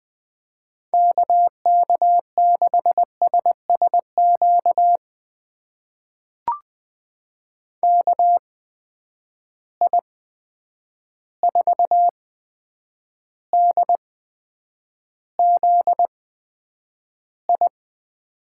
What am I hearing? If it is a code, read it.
Morse code, heard as a keyed tone, about 20 wpm, spaced as for 9 wpm, 700 Hz. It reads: KK6SSQ KI4DZI
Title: USA Call Signs - 6 Characters - 8x Character Spacing - 20wpm